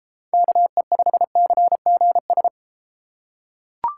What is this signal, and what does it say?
Morse code, heard as a keyed tone, about 32 wpm, 700 Hz. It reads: KE5CGS